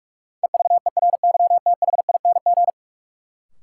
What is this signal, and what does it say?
Morse code, heard as a keyed tone, about 46 wpm, 700 Hz. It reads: EVERYTHING
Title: Sentences from Top 700 Words - New Word Drill - 45wpm